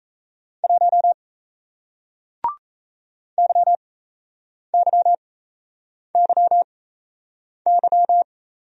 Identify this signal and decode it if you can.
Morse code, heard as a keyed tone, about 37 wpm, spaced as for 8 wpm, 700 Hz. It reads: 1 YYYY